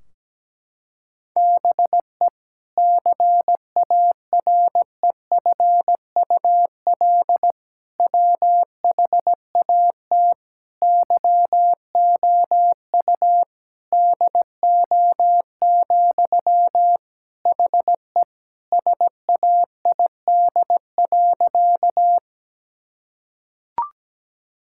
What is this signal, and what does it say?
Morse code, heard as a keyed tone, about 17 wpm, 700 Hz. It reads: BE CAREFUL WHAT YOU DO, HE SAID.
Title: Sentences from Top 1600 Words 17wpm